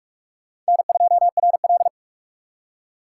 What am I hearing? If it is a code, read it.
Morse code, heard as a keyed tone, about 61 wpm, spaced as for 53 wpm, 700 Hz. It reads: N2RL
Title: Speed Racing - USA 1x2 Call Signs 45wpm